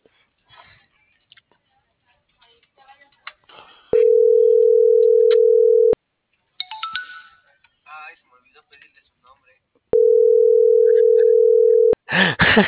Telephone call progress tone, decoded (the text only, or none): ringback tone